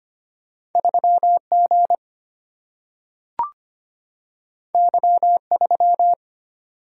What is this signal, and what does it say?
Morse code, heard as a keyed tone, about 25 wpm, 700 Hz. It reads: 3G Y3
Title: Single-Letter Single-Number 25wpm